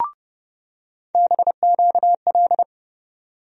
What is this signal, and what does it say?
Morse code, heard as a keyed tone, about 30 wpm, 700 Hz. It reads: BQL
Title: USA Call Sign Postfixes - 3 Letters 30wpm